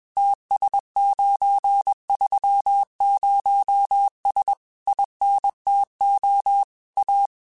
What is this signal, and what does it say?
Morse code, heard as a keyed tone, about 21 wpm, 800 Hz. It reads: TS930S INTO A
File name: QSOs / QSOs-mp3s / kc7sac@20